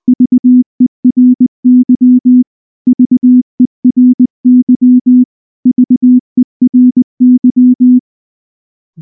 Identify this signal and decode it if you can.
Morse code, heard as a keyed tone, about 20 wpm, 250 Hz. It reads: VERY VERY VERY
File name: morse / morse-training-20WPM-5